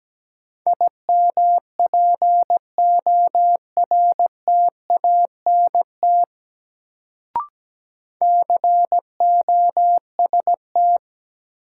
Morse code, heard as a keyed tone, about 17 wpm, 700 Hz. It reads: IMPORTANT COST
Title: Sets of 3 Words - New Word Drill - 17wpm